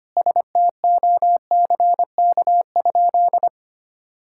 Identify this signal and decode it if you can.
Morse code, heard as a keyed tone, about 25 wpm, 700 Hz. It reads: STOCK?
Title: Sentences from Top 1000 Words - Review Part 5 - 25wpm